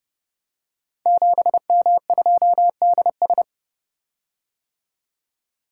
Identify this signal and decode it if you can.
Morse code, heard as a keyed tone, about 30 wpm, 700 Hz. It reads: 7M2DS
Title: Code Groups - 5 Letter:Numbers - 30wpm